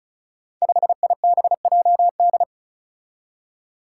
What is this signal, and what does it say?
Morse code, heard as a keyed tone, about 36 wpm, 700 Hz. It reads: 5IBJD
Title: Five Challenging Letter-Number Combinations 35wpm